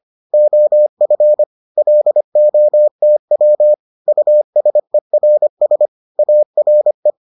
Morse code, heard as a keyed tone, about 25 wpm, 600 Hz. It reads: OF LOTW USERS ARE